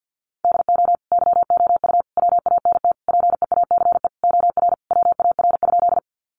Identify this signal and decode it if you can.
Morse code, heard as a keyed tone, about 50 wpm, 700 Hz. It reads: DO YOU WANT PEACE OR WAR?
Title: Sentences from Top 1500 Words 50wpm